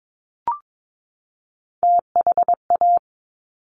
Morse code, heard as a keyed tone, about 22 wpm, 700 Hz. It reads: THA
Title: Top Three Letter Combinations in Words 22wpm